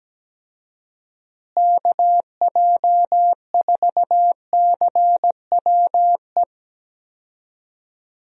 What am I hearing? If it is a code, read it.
Morse code, heard as a keyed tone, about 17 wpm, 700 Hz. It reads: KJ4CWE
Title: USA Call Signs - 6 Characters - Repeated 2x - 17wpm